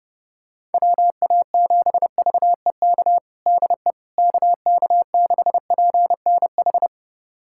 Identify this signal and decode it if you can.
Morse code, heard as a keyed tone, about 30 wpm, 700 Hz. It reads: WA7VEK DE KK6PNH
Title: USA Callsign DE Callsign - 6 Characters - 30wpm